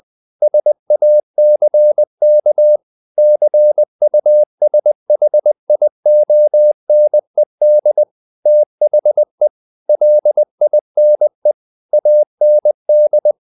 Morse code, heard as a keyed tone, about 20 wpm, 600 Hz. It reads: SACK CUSHIONED THE LINE AND
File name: the old man and the sea / the old man and the sea part 1 20wpm